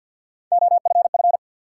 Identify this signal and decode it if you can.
Morse code, heard as a keyed tone, about 52 wpm, 700 Hz. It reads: OFF
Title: Sentences from Top 200 Words - New Word Drill - 50wpm